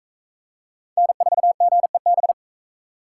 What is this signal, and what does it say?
Morse code, heard as a keyed tone, about 37 wpm, 700 Hz. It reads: N4GEB